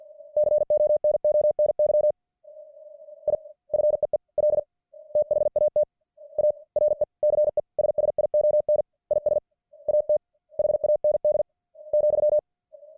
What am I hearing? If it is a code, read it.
Morse code, heard as a keyed tone, about 50 wpm, 600 Hz. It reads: CONONY I FEEL THAT A RECESSION IS AT HAND ,